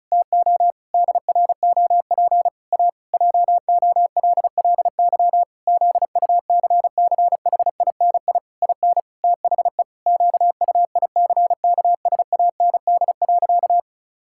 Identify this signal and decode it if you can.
Morse code, heard as a keyed tone, about 36 wpm, 700 Hz. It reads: TO DROP A JOLLY ZUCCHINI IN THE QUICKSAND.
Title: Pangram Challenge v2 - 35wpm